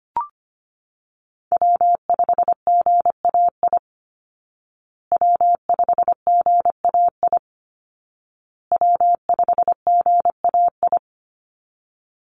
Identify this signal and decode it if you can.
Morse code, heard as a keyed tone, about 25 wpm, 700 Hz. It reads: W5GAI W5GAI W5GAI
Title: Call Signs Repeated 2x - 25wpm